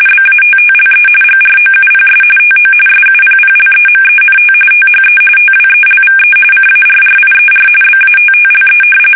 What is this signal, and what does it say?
RTTY, about 45.45 baud. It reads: BATTLE GROUPS TO SUPPLEMENT THE TWELVE CARRIER BATTLE
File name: fsk850 / battleshipnj